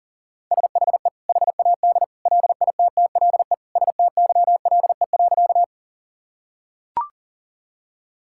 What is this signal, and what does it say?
Morse code, heard as a keyed tone, about 40 wpm, 700 Hz. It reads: SHE HAD LITTLE STYLE.
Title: Sentences from Top 1000 Words - Review Part 5 - 40wpm